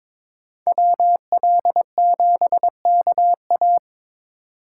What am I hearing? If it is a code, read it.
Morse code, heard as a keyed tone, about 26 wpm, 700 Hz. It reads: WL7KA
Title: Speed Racing - USA Call Signs 22wpm